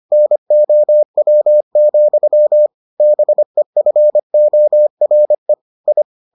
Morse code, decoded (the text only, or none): NOW, BEFORE I